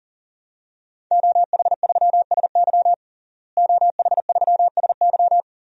Morse code, heard as a keyed tone, about 40 wpm, 700 Hz. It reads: OH3SY OH3SY
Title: DXCC-100 Call Signs - 5 Characters - Repeated 2x - 40wpm